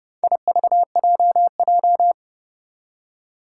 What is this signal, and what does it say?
Morse code, heard as a keyed tone, about 30 wpm, 700 Hz. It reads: IVJJ
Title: Four Challenging Letter-Number Combinations 30wpm